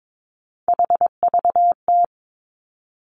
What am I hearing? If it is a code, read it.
Morse code, heard as a keyed tone, about 22 wpm, 700 Hz. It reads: HVT
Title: Three Challenging Letter-Number Combinations 22wpm